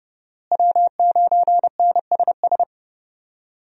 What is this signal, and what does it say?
Morse code, heard as a keyed tone, about 30 wpm, 700 Hz. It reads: W9NSS
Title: USA Call Signs - 5 Characters - Repeated 3x - 30wpm